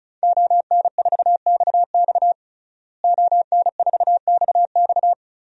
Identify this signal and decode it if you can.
Morse code, heard as a keyed tone, about 34 wpm, 700 Hz. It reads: ON4XX ON4XX
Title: DXCC-100 Call Signs - 5 Characters - Repeated 2x - 35wpm